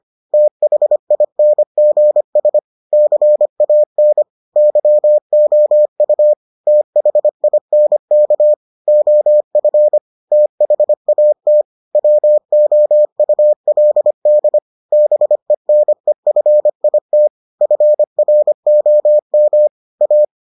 Morse code, heard as a keyed tone, about 25 wpm, 600 Hz. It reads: THINGS CAN YOU THINK OF THAT WOULD BENEFIT FROM A